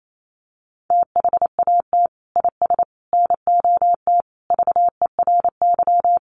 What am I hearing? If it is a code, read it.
Morse code, heard as a keyed tone, about 28 wpm, 700 Hz. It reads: THAT IS NOT VERY
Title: Sentences from Top 600 Words 28wpm